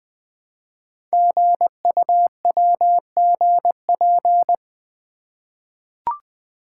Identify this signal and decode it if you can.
Morse code, heard as a keyed tone, about 20 wpm, 700 Hz. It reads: GUWGP
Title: Code Groups - 5 Letters - 20wpm